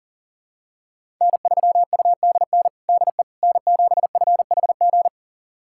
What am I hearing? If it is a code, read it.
Morse code, heard as a keyed tone, about 37 wpm, 700 Hz. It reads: N3UDN DE N7FHG
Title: USA Callsign DE Callsign - 5 Characters - 40wpm